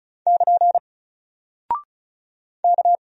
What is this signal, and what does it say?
Morse code, heard as a keyed tone, about 36 wpm, 700 Hz. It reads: <KN> K
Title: Single Character (Letter:Number:Punctuation:Prosign) 35wpm